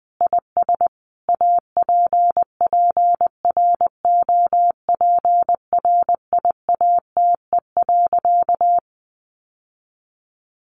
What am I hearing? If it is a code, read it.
Morse code, heard as a keyed tone, about 20 wpm, 700 Hz. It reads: IS APPROPRIATE.
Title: Sentences from Top 900 Words 20wpm